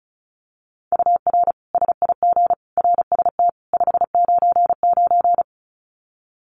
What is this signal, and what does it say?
Morse code, heard as a keyed tone, about 34 wpm, 700 Hz. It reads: UR SIG RST 599
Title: Common QSO Phrases 35wpm